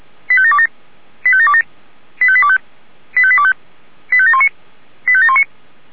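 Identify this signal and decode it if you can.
Five-tone selective call (ZVEI), heard as quick runs of five tones, tones of about 70 ms. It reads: 85727 85727 85825 85825 85719 85719
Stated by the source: ZVEI five-tone call received by SDR.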